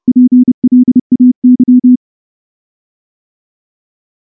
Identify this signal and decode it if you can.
Morse code, heard as a keyed tone, about 29 wpm, 250 Hz. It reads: PLAY
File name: morse / morse-training-30WPM